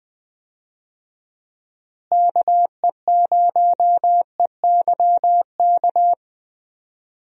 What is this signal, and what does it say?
Morse code, heard as a keyed tone, about 20 wpm, 700 Hz. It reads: KE0EYK
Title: USA Call Signs - 6 Characters - Repeated 3x - 20wpm